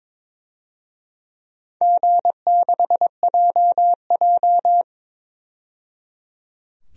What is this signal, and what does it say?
Morse code, heard as a keyed tone, about 22 wpm, 700 Hz. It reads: G6JJ
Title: DXCC-100 Call Signs - 4 Characters - 22wpm